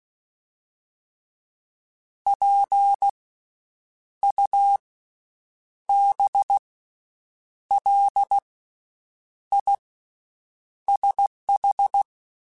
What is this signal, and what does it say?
Morse code, heard as a keyed tone, about 16 wpm, spaced as for 7 wpm, 800 Hz. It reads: PUBLISH
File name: Special INGs / Special INGs-mp3s / VerySlowToFaster Ing @15